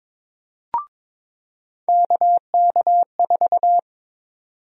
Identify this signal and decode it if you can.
Morse code, heard as a keyed tone, about 22 wpm, 700 Hz. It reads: KK4
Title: USA Call Sign Prefixes - 22wpm